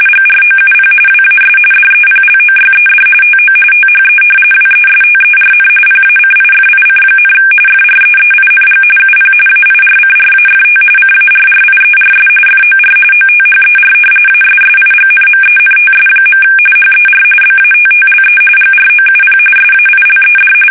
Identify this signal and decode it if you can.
RTTY, about 45.45 baud. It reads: GROUP WAS DEPLOYED TO THE WESTERN PACIFIC, FREEING UP AIRCRAFT CARRIERS FOR OTHER ASSIGNMENTS. HER BATTLE GROUP SAILED
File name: fsk850 / battleshipnj